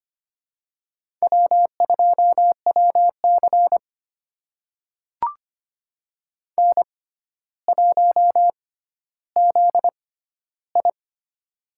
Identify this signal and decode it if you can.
Morse code, heard as a keyed tone, about 25 wpm, spaced as for 13 wpm, 700 Hz. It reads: W2WC N1ZI
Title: USA Call Signs - 6x Character Spacing - 25wpm